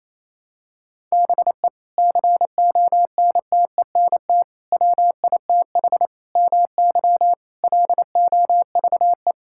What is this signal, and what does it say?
Morse code, heard as a keyed tone, about 28 wpm, 700 Hz. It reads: BE CONTENT WITH MY LOVE
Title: Sentences from Top 1000 Words - Review Part 6 - 28wpm